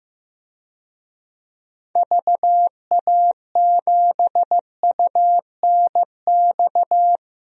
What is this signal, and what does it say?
Morse code, heard as a keyed tone, about 16 wpm, 700 Hz. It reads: VA7UNX
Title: DXCC-100 Call Signs - 6 Characters - Repeated 2x - 15wpm